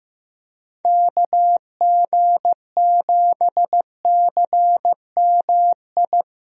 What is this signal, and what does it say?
Morse code, heard as a keyed tone, about 15 wpm, 700 Hz. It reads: KG7CMI
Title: USA Call Signs - 6 Characters - 15wpm